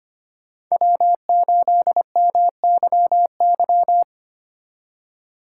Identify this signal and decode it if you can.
Morse code, heard as a keyed tone, about 33 wpm, 700 Hz. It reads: W8MYY